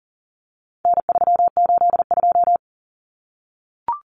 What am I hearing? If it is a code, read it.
Morse code, heard as a keyed tone, about 40 wpm, 700 Hz. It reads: N382
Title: Common Cut Numbers - 4-Digit 40wpm